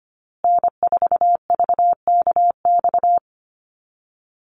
Morse code, heard as a keyed tone, about 25 wpm, 700 Hz. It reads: N4VKX
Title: USA Call Signs Repeated 3x 25wpm